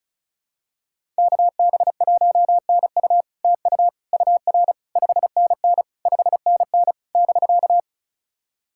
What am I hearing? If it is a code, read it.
Morse code, heard as a keyed tone, about 36 wpm, 700 Hz. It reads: KB1NU TU UR 5NN 5NN <BK>
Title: SOTA QSO Practice 35wpm